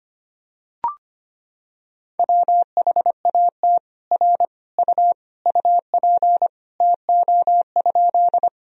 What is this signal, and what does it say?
Morse code, heard as a keyed tone, about 25 wpm, 700 Hz. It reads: WHAT R U UP TO?
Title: Sentences from Top 100 Words and Abbreviations 25wpm